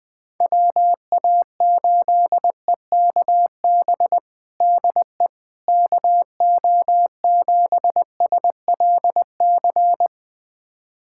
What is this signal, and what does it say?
Morse code, heard as a keyed tone, about 19 wpm, 700 Hz. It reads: WA8EKB DE KO7SLC
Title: USA Callsign DE Callsign - 6 Characters - 20wpm